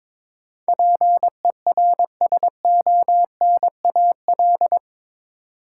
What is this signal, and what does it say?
Morse code, heard as a keyed tone, about 22 wpm, 700 Hz. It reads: PERSONAL
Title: Sentences from Top 700 Words - New Word Drill - 22wpm